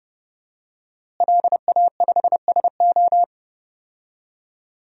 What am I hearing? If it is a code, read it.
Morse code, heard as a keyed tone, about 30 wpm, 700 Hz. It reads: LA5SO
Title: DXCC-100 Call Signs - 5 Characters - Repeated 2x - 30wpm